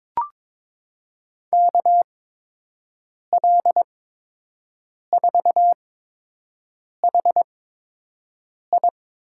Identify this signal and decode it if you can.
Morse code, heard as a keyed tone, about 22 wpm, spaced as for 6 wpm, 700 Hz. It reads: KL4HI
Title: USA Call Signs - 5 Characters - 8x Character Spacing - 22wpm